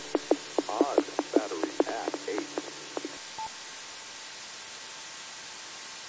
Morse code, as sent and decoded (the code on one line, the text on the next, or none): none